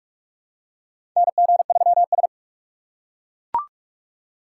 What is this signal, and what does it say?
Morse code, heard as a keyed tone, about 42 wpm, 700 Hz. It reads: NG3S
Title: USA Call Signs - 4 Characters - 4x Character Spacing - 45wpm